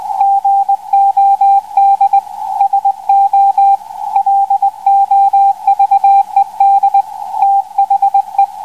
Morse code, heard as a keyed tone, about 19 wpm, 790 Hz, with radio noise. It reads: GOD SO LOVED THE